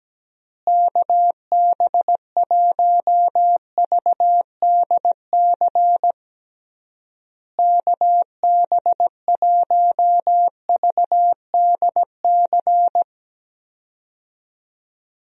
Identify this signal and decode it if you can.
Morse code, heard as a keyed tone, about 17 wpm, 700 Hz. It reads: KB1VDC KB1VDC
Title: USA Call Signs - 6 Characters - Repeated 3x - 17wpm